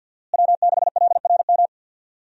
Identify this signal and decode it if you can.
Morse code, heard as a keyed tone, about 51 wpm, 700 Hz. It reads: W6LRM
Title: USA Call Signs - 5 Characters - Repeated 3x - 50wpm